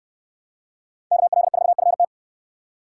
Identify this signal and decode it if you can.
Morse code, heard as a keyed tone, about 69 wpm, 700 Hz. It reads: KL4FT